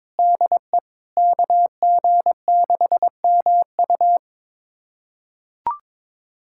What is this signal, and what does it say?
Morse code, heard as a keyed tone, about 23 wpm, 700 Hz. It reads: DE KG6MU
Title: USA Callsign DE Callsign - 5 Characters - 22wpm